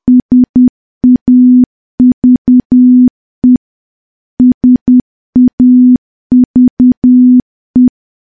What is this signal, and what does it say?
Morse code, heard as a keyed tone, about 10 wpm, 250 Hz. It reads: SAVE SAVE
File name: morse / morse-training-10WPM